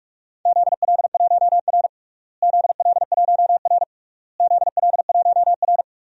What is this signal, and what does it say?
Morse code, heard as a keyed tone, about 46 wpm, 700 Hz. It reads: ZL1R ZL1R ZL1R
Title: DXCC-100 Call Signs - 4 Characters - Repeated 3x - 45wpm